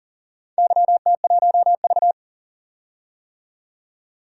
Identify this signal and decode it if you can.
Morse code, heard as a keyed tone, about 40 wpm, 700 Hz. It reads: YT1V
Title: DXCC-100 Call Signs - 4 Characters - 40wpm